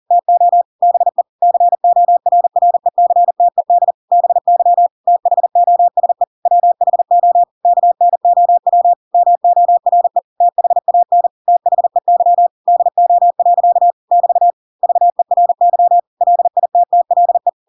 Morse code, heard as a keyed tone, about 40 wpm, 700 Hz. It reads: TO BE CORRECTED BY THOSE WHO KNOW MORE THAN THEY DO. <BT> VERY LITTLE
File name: Pinocchio / Pinocchio ch4 40 40 ik1hge